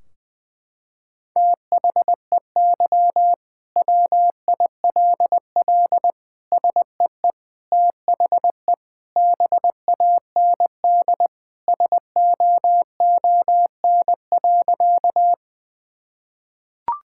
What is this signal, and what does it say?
Morse code, heard as a keyed tone, about 20 wpm, 700 Hz. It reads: THEY WILL SEE THE BAND SOON.
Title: Sentences from Top 1600 Words 20wpm